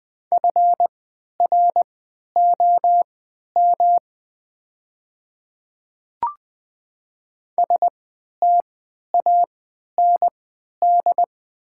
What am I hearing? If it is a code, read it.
Morse code, heard as a keyed tone, about 20 wpm, spaced as for 11 wpm, 700 Hz. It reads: FROM STAND
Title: 3x Character Spacing - Top 300 Words - 20wpm